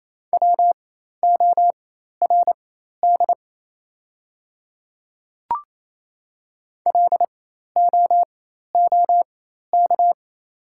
Morse code, heard as a keyed tone, about 28 wpm, spaced as for 13 wpm, 690 Hz. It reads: WORD LOOK
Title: 4x Character Spacing - Top 200 Words - 28wpm